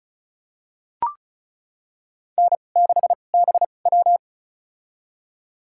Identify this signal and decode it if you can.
Morse code, heard as a keyed tone, about 36 wpm, spaced as for 25 wpm, 700 Hz. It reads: N6BW
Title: USA Call Signs - 4 Characters - 2x Character Spacing - 35wpm